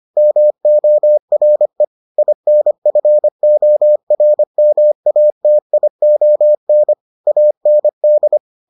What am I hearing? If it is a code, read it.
Morse code, heard as a keyed tone, about 25 wpm, 600 Hz. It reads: MORE INFORMATION AND